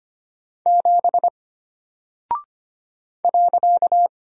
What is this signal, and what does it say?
Morse code, heard as a keyed tone, about 25 wpm, 700 Hz. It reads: 7 .
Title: Single Character (Letter:Number:Punctuation:Prosign) Alt 25wpm